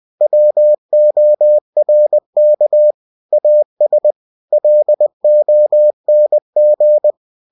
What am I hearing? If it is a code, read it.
Morse code, heard as a keyed tone, about 20 wpm, 600 Hz. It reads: WORK AS LONG
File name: cw-books / winnie-the-pooh / winnie-20wpm-page-0114